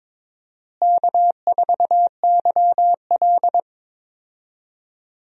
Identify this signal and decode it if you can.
Morse code, heard as a keyed tone, about 22 wpm, 700 Hz. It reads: K4YL
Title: USA Call Signs - 4 Characters - Repeated 2x - 22wpm